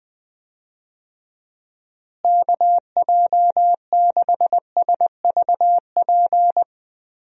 Morse code, heard as a keyed tone, about 20 wpm, 700 Hz. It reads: KJ6SVP
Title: USA Call Signs - 6 Characters - Repeated 2x - 20wpm